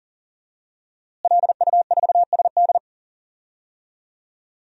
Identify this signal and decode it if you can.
Morse code, heard as a keyed tone, about 40 wpm, 700 Hz. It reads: LU4SD
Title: DXCC-100 Call Signs - 5 Characters - Repeated 2x - 40wpm